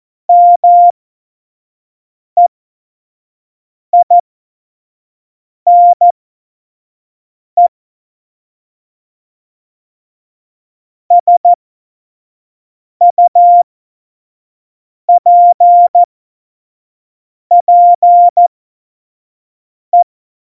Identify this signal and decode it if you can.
Morse code, heard as a keyed tone, about 14 wpm, spaced as for 5 wpm, 700 Hz. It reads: MEINE SUPPE
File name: Morsekurs / Witze / Witz 24. September 2019, 25 F70